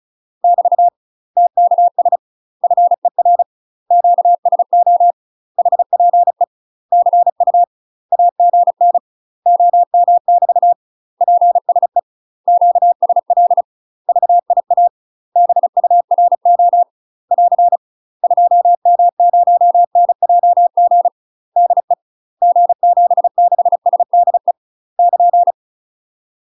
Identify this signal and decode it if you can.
Morse code, heard as a keyed tone, about 36 wpm, 700 Hz. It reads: <BT> TKS FER QSO HPE CU AGN OM <BT> PSE QSL VIA BURO <AR> 2M0NJG DE G76SDE <KN>
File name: UK MASTER QSOs TRAINING / 35WPM / UK MASTER QSOs 0011